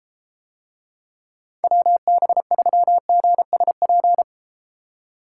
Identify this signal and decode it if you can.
Morse code, heard as a keyed tone, about 30 wpm, 700 Hz. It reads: WB3GSP